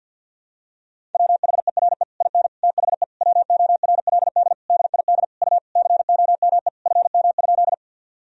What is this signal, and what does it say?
Morse code, heard as a keyed tone, about 50 wpm, 700 Hz. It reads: WHERE IN THE WORLD DID U COME FM?
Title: Sentences from Top 200 Words and Abbreviations 50wpm